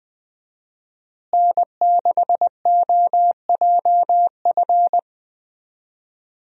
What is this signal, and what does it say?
Morse code, heard as a keyed tone, about 20 wpm, 700 Hz. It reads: N6OJF
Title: USA Call Signs - 5 Characters - Repeated 2x - 20wpm